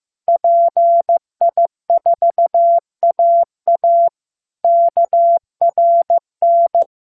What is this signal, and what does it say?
Morse code, heard as a keyed tone, about 15 wpm, 680 Hz. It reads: PI4AA KRN
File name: pi4aa / PI4AA Uitzending